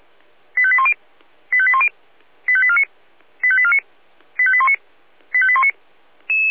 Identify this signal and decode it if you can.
Five-tone selective call (ZVEI), heard as quick runs of five tones, tones of about 70 ms. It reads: 85810 85810 85849 85849 85819 85819
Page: SDR/ZVEI